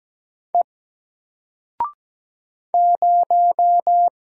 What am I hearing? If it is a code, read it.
Morse code, heard as a keyed tone, about 17 wpm, 700 Hz. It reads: E 0
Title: Single Character (Letter:Number:Punctuation:Prosign) 17wpm